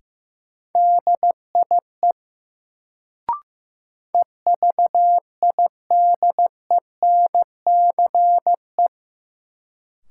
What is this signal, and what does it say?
Morse code, heard as a keyed tone, about 15 wpm, 700 Hz. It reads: DIE EVIDENCE
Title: Sentences from Top 1300 Words - New Word Drill Alt - 15wpm